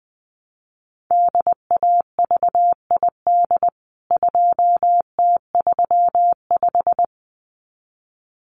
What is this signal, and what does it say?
Morse code, heard as a keyed tone, about 20 wpm, 700 Hz. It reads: DAVID 2T35
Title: CWT Exchange 20wpm